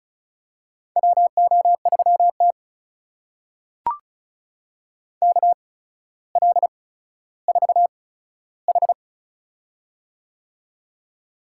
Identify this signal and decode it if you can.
Morse code, heard as a keyed tone, about 36 wpm, spaced as for 15 wpm, 700 Hz. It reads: WO3T KL4H
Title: USA Call Signs - 4 Characters - 8x Character Spacing - 35wpm